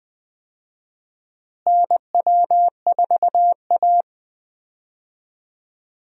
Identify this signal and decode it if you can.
Morse code, heard as a keyed tone, about 20 wpm, 700 Hz. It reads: NW4A
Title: USA Call Signs - 4 Characters - Repeated 3x - 20wpm